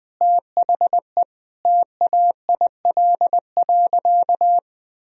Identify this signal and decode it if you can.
Morse code, heard as a keyed tone, about 21 wpm, 700 Hz. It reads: THE TAIL.
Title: Sentences from Top 1000 Words - Review Part 3 - 20wpm